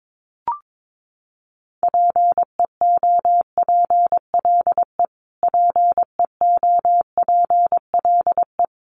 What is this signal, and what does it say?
Morse code, heard as a keyed tone, about 22 wpm, 700 Hz. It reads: PEOPLE PEOPLE
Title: Top 200 Words Repeated 22wpm